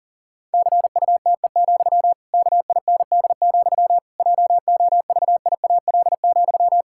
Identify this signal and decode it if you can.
Morse code, heard as a keyed tone, about 37 wpm, 700 Hz. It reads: CUTE, KIND, JOVIAL,
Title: Pangram Challenge v2 - 40wpm